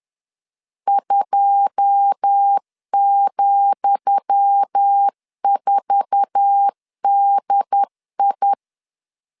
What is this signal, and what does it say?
Morse code, heard as a keyed tone, about 10 wpm, 800 Hz. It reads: 2,4DI
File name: LEZIONI / 10 SIGNS / 10D COMA 10wpm 3013